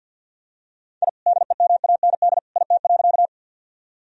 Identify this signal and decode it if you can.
Morse code, heard as a keyed tone, about 52 wpm, 700 Hz. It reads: I DEMAND IT.
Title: Sentences from Top 1000 Words - Review Part 4 - 50wpm